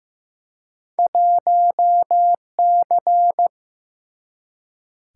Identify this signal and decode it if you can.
Morse code, heard as a keyed tone, about 15 wpm, 700 Hz. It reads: 1C
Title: Single-Letter Single-Number 15wpm